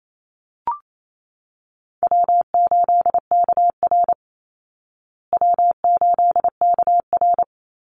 Morse code, heard as a keyed tone, about 28 wpm, 700 Hz. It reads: W8KR W8KR
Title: Easy Call Signs Repeated 2x - 28wpm